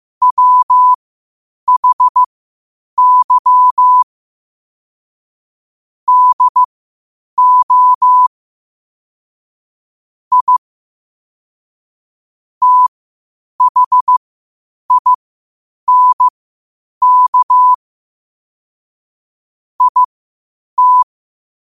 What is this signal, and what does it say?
Morse code, heard as a keyed tone, about 15 wpm, spaced as for 8 wpm, 1000 Hz. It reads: WHY DO I THINK IT
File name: qotdfiles / 8wpm